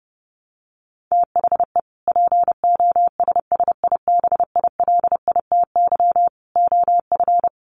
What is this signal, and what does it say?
Morse code, heard as a keyed tone, about 30 wpm, 700 Hz. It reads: THE POSSIBILITY OF
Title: Sets of 3 Words 30wpm